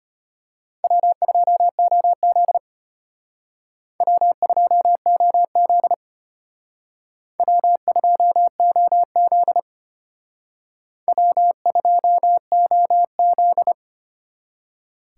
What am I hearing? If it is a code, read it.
Morse code, heard as a keyed tone, about 32 wpm, 700 Hz. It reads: W2OZ W2OZ W2OZ W2OZ